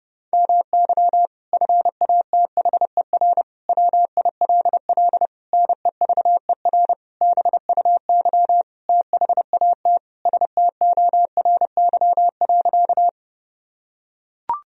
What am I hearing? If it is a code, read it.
Morse code, heard as a keyed tone, about 32 wpm, 700 Hz. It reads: MY FATHER WILL NEVER BUY THAT STORY.
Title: Sentences from Top 1000 Words - Review Part 4 - 30wpm